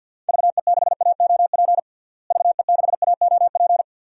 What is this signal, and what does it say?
Morse code, heard as a keyed tone, about 53 wpm, 700 Hz. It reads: VE6AOP VE6AOP
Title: DXCC-100 Call Signs - 6 Characters - Repeated 2x - 50wpm